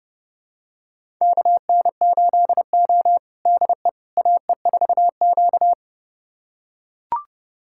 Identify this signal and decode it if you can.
Morse code, heard as a keyed tone, about 30 wpm, 700 Hz. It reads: KN8O DE AE4Q
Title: USA Callsign DE Callsign - 4 Characters - 30wpm